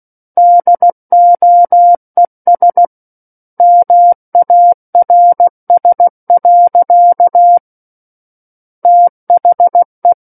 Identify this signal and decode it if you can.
Morse code, heard as a keyed tone, about 16 wpm, 700 Hz. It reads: DOES MARS. THE
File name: Book One / 07-Chapter 7 - 16 WPM